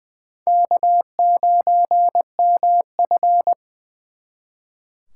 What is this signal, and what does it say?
Morse code, heard as a keyed tone, about 23 wpm, 700 Hz. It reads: K9MF
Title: Speed Racing - USA 1x2 Call Signs 20wpm